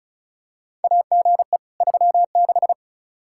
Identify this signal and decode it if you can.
Morse code, heard as a keyed tone, about 36 wpm, 700 Hz. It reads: AGE 36
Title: Practice QSO - Exchange 3 - More Info - 35wpm